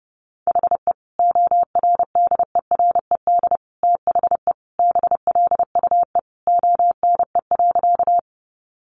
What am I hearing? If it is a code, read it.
Morse code, heard as a keyed tone, about 30 wpm, 700 Hz. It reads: HE ORDERED THE BLUE ONE.
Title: Additional Easy Phrases 30wpm